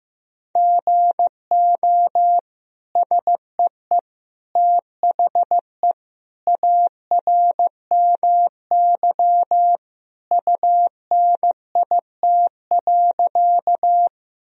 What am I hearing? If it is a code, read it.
Morse code, heard as a keyed tone, about 15 wpm, 700 Hz. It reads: GO SEE THE ARMY UNIT.
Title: Sentences from Top 1000 Words - Review Part 2 - 15wpm